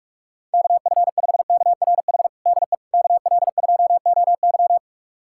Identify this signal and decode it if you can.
Morse code, heard as a keyed tone, about 46 wpm, 700 Hz. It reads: KF5KRH DE KL2OY
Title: USA Callsign DE Callsign 45wpm